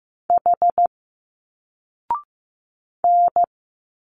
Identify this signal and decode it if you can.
Morse code, heard as a keyed tone, about 15 wpm, 700 Hz. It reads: H N
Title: Single Letters 15wpm